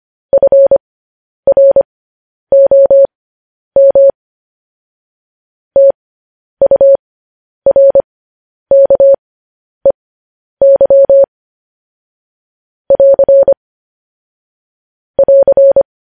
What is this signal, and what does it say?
Morse code, heard as a keyed tone, about 25 wpm, spaced as for 10 wpm, 550 Hz. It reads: FROM TURKEY <AR> <AR>
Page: news-headlines-in-morse-code-at-10-wpm